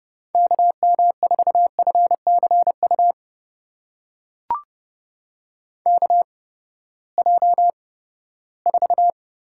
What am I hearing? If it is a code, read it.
Morse code, heard as a keyed tone, about 30 wpm, spaced as for 17 wpm, 700 Hz. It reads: KM4FCU KJ4
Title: USA Call Signs - 6 Characters - 8x Character Spacing - 30wpm